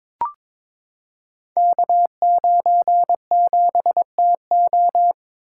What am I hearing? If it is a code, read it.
Morse code, heard as a keyed tone, about 22 wpm, 700 Hz. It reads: K97TO
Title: Code Groups - 5 Letter:Numbers - 22wpm